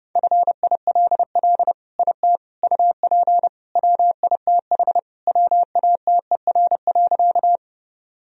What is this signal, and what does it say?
Morse code, heard as a keyed tone, about 30 wpm, 700 Hz. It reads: FILL IT UP WITH WATER.
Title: Sentences from Top 1000 Words - Review Part 3 - 30wpm